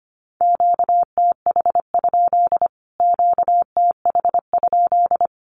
Morse code, heard as a keyed tone, about 25 wpm, 700 Hz. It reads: QTH? QTH?
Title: Q-Codes Repeated 3x 25wpm